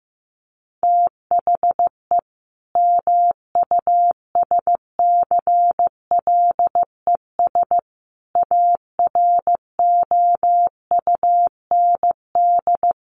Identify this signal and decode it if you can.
Morse code, heard as a keyed tone, about 15 wpm, 700 Hz. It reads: THE MUSCLES AROUND
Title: A Sound Of Thunder 15wpm